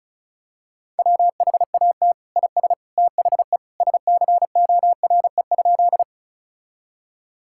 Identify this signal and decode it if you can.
Morse code, heard as a keyed tone, about 34 wpm, 700 Hz. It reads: WHAT IS THE SCORE?
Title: Sentences from Top 1600 Words 35wpm